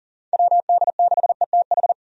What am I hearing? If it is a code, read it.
Morse code, heard as a keyed tone, about 37 wpm, 700 Hz. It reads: WD6ETH
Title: USA Call Signs - 6 Characters - 40wpm